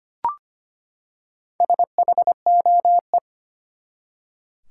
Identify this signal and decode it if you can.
Morse code, heard as a keyed tone, about 25 wpm, 700 Hz. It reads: SHOE
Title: Four Letter Words - 25wpm